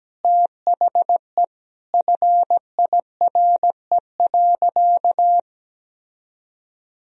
Morse code, heard as a keyed tone, about 17 wpm, 700 Hz. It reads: THE FIRE.
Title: Sentences from Top 1000 Words - Review Part 1 - 17wpm